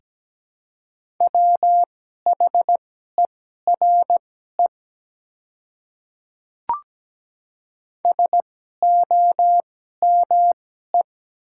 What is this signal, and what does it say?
Morse code, heard as a keyed tone, about 17 wpm, spaced as for 12 wpm, 700 Hz. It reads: WHERE SOME
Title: 2x Character Spacing - Top 200 Words - 17wpm